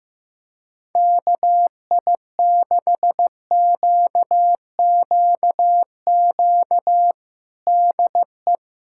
Morse code, heard as a keyed tone, about 15 wpm, 700 Hz. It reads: KI6QQQ DE
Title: USA Callsign DE Callsign 15wpm